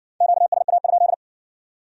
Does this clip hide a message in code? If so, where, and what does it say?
Yes: Morse code from 0.2 to 1.1 s, heard as a keyed tone, about 56 wpm, 700 Hz. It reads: QSA?